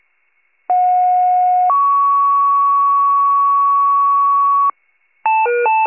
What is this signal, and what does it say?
Two-tone page, heard as a one-second tone, then a three-second tone, about 730 then 1120 Hz. It is heard from 0.7 to 4.7 s.